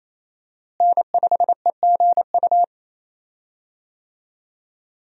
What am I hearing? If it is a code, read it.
Morse code, heard as a keyed tone, about 28 wpm, 700 Hz. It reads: N5EGU
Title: USA Call Signs - 5 Characters - Repeated 3x - 28wpm